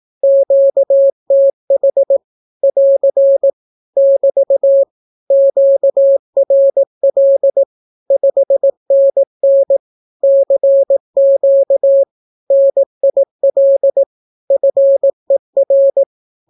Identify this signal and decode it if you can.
Morse code, heard as a keyed tone, about 18 wpm, 550 Hz. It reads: QTH <AR> <BT> QRL 5NN CQ NIL FER
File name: naukapodstaw / skroty / skroty